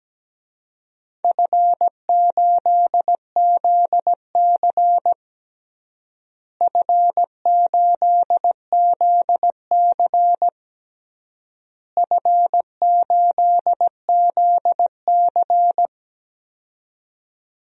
Morse code, heard as a keyed tone, about 17 wpm, 700 Hz. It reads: F8ZC F8ZC F8ZC
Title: DXCC-100 Call Signs - 4 Characters - Repeated 3x - 17wpm